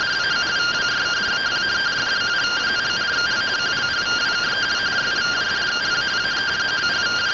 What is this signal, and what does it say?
Two-tone teleprinter signal, about 170 Hz shift.